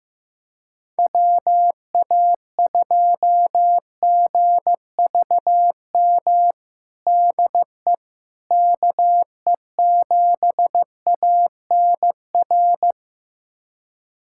Morse code, heard as a keyed tone, about 15 wpm, 700 Hz. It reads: WA2GVM DE KE7ANR
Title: USA Callsign DE Callsign - 6 Characters - 15wpm